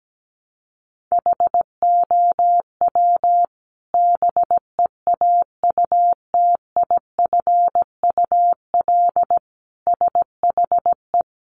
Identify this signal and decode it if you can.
Morse code, heard as a keyed tone, about 17 wpm, 700 Hz. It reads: HOW BEAUTIFUL SHE
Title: Sentences from Top 900 Words 17wpm